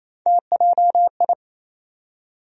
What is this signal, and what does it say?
Morse code, heard as a keyed tone, about 28 wpm, 700 Hz. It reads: TJI
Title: USA Call Sign Postfixes - 3 Letters 28wpm